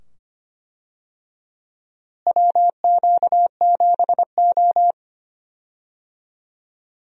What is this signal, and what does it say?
Morse code, heard as a keyed tone, about 25 wpm, 700 Hz. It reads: WQ7O
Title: USA Call Signs - 4 Characters - Repeated 2x - 25wpm